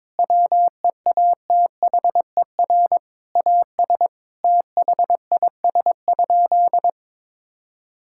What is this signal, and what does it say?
Morse code, heard as a keyed tone, about 22 wpm, 700 Hz. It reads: WEATHER AS THIS?
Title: Sentences from Top 1000 Words - Review Part 5 - 22wpm